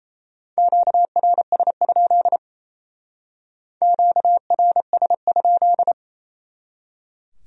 Morse code, heard as a keyed tone, about 35 wpm, 700 Hz. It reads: QRS? QRS?